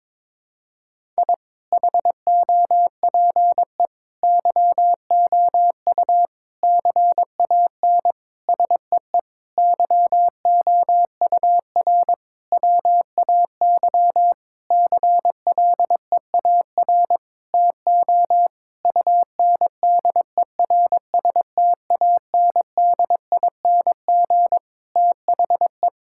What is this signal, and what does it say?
Morse code, heard as a keyed tone, about 22 wpm, 700 Hz. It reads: I HOPE YOU CAN SEE YOUR WAY CLEAR TO UNDERSTANDING THE
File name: Sentences with 9-Word Phrases - 22wpm